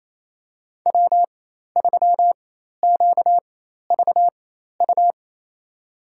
Morse code, heard as a keyed tone, about 28 wpm, spaced as for 13 wpm, 700 Hz. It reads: W3QVU
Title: USA Call Signs - 5 Characters - 4x Character Spacing - 28wpm